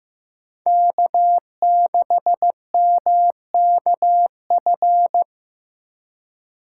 Morse code, heard as a keyed tone, about 15 wpm, 700 Hz. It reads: K6MKF
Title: K1USN SST QSO Practice - 15wpm